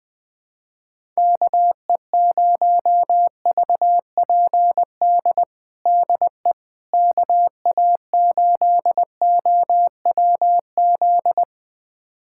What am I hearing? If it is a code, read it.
Morse code, heard as a keyed tone, about 21 wpm, 700 Hz. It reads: KE0VPD DE KA8OWZ
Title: USA Callsign DE Callsign - 6 Characters - 20wpm